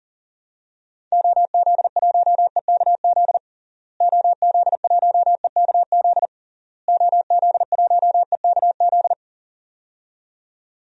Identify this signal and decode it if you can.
Morse code, heard as a keyed tone, about 40 wpm, 700 Hz. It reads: OZ1EKZ OZ1EKZ OZ1EKZ
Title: DXCC-100 Call Signs - 6 Characters - Repeated 3x - 40wpm